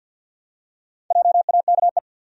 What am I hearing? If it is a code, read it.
Morse code, heard as a keyed tone, about 52 wpm, 700 Hz. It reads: JAKE
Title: USA Common Names - 50wpm